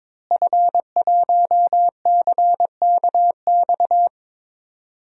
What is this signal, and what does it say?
Morse code, heard as a keyed tone, about 22 wpm, 700 Hz. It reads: F1CKX
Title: DXCC-100 Call Signs - 5 Characters - Repeated 2x - 22wpm